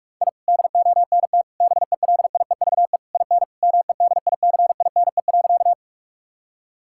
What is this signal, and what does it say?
Morse code, heard as a keyed tone, about 46 wpm, 700 Hz. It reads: I DONT BELIEVE IN MEDICINE.
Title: Sentences from Top 1000 Words - Review Part 6 - 45wpm